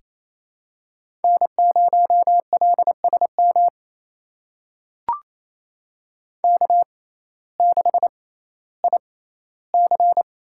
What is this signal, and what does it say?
Morse code, heard as a keyed tone, about 28 wpm, spaced as for 15 wpm, 700 Hz. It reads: N0LSM K6IC
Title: USA Call Signs - 6x Character Spacing - 28wpm